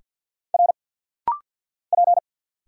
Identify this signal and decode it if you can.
Morse code, heard as a keyed tone, about 50 wpm, 700 Hz. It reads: R P
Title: Single Letters - Rapid-Fire 50wpm